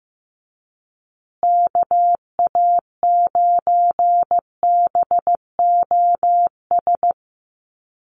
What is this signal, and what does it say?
Morse code, heard as a keyed tone, about 15 wpm, 700 Hz. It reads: KA9BOS
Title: USA Call Signs Repeated 3x 15wpm